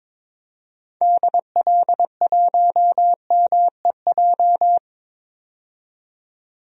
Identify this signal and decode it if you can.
Morse code, heard as a keyed tone, about 22 wpm, 700 Hz. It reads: DL1MEJ
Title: DXCC-100 Call Signs - 6 Characters - 22wpm